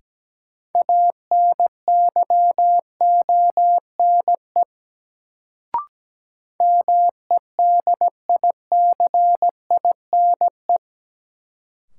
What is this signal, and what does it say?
Morse code, heard as a keyed tone, about 17 wpm, 700 Hz. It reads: ANYONE MEDICINE
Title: Sentences from Top 900 Words - New Word Drill - 17wpm